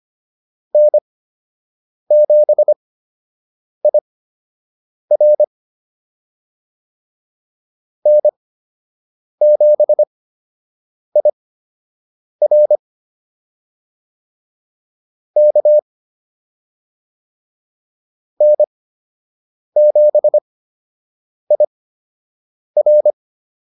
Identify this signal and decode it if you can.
Morse code, heard as a keyed tone, about 25 wpm, spaced as for 7 wpm, 600 Hz. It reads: N7IR N7IR K N7IR